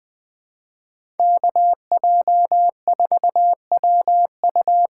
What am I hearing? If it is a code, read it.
Morse code, heard as a keyed tone, about 20 wpm, 700 Hz. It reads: KJ4WU
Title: USA Call Signs - 6 Characters - 6x Character Spacing - 20wpm